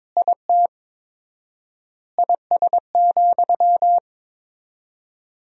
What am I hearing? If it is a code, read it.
Morse code, heard as a keyed tone, about 22 wpm, 700 Hz. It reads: IT IS,
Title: 4x Word-Spacing - Sentences from Top 200 Words - 22wpm